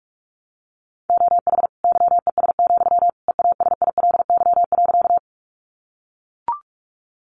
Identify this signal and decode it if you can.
Morse code, heard as a keyed tone, about 44 wpm, 700 Hz. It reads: OH YES, EASILY.
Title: Sentences from Top 900 Words 45wpm